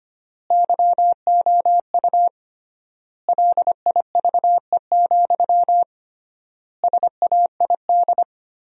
Morse code, heard as a keyed tone, about 25 wpm, 700 Hz. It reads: YOU LIVE, SAID
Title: 3x Word-Spacing - Sentences from Top 200 Words - 25wpm